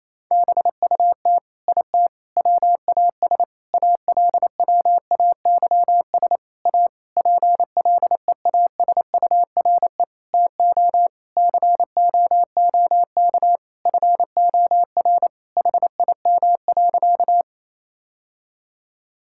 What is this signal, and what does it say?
Morse code, heard as a keyed tone, about 28 wpm, 700 Hz. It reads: BUT IT WAS ALWAYS A PLEASURE TO COOK FOR HIM.
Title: Sentences from Top 1600 Words 28wpm